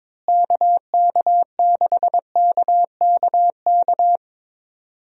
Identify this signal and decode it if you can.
Morse code, heard as a keyed tone, about 22 wpm, 700 Hz. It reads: KK6KKK
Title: USA Call Signs - 6 Characters - Repeated 3x - 22wpm